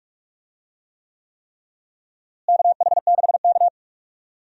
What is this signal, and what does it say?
Morse code, heard as a keyed tone, about 42 wpm, 700 Hz. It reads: KH6K
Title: DXCC-100 Call Signs - 4 Characters - Repeated 3x - 45wpm